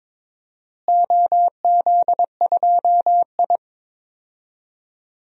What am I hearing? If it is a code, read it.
Morse code, heard as a keyed tone, about 22 wpm, 700 Hz. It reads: OZ2I
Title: DXCC-100 Call Signs - 4 Characters - Repeated 3x - 22wpm